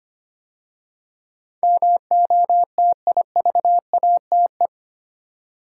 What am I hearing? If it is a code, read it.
Morse code, heard as a keyed tone, about 25 wpm, 700 Hz. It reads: MOTIVATE
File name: 8 Letter Words - First 3 Letters Given - 25wpm